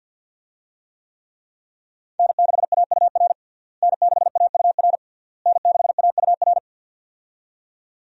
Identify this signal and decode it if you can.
Morse code, heard as a keyed tone, about 49 wpm, 700 Hz. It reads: N6AUR N6AUR N6AUR
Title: USA Call Signs - 5 Characters - Repeated 3x - 50wpm